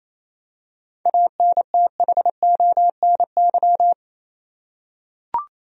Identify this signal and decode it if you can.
Morse code, heard as a keyed tone, about 28 wpm, 700 Hz. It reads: ANTHONY
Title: USA Common Names - 28wpm